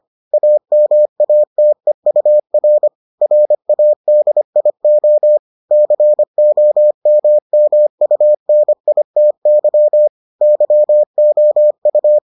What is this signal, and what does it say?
Morse code, heard as a keyed tone, about 25 wpm, 600 Hz. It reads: AMATEUR RADIO COMMUNITY YOU